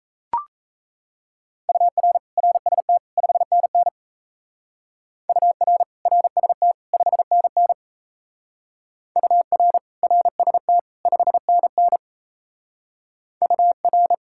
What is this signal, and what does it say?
Morse code, heard as a keyed tone, about 36 wpm, 700 Hz. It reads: UR RST 5NN UR RST 5NN UR RST 5NN UR